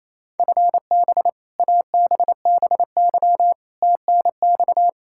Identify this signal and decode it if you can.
Morse code, heard as a keyed tone, about 28 wpm, 700 Hz. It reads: FB ABBY TNX
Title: Practice QSO - Exchange 3 - More Info - 28wpm